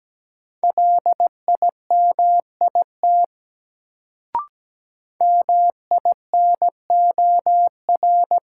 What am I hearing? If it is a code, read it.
Morse code, heard as a keyed tone, about 17 wpm, 700 Hz. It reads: LIMIT MINOR
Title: Sentences from Top 1500 Words - New Word Drill - 17wpm